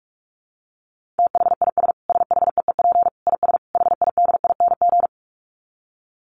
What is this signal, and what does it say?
Morse code, heard as a keyed tone, about 44 wpm, 700 Hz. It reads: THIS SHEEP IS HIDING
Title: Dit Challenge 45wpm